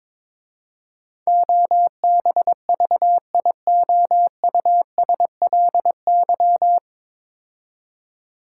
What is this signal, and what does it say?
Morse code, heard as a keyed tone, about 22 wpm, 700 Hz. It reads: OBVIOUSLY
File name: 9 Letter Words - First 3 Letters Given - 22wpm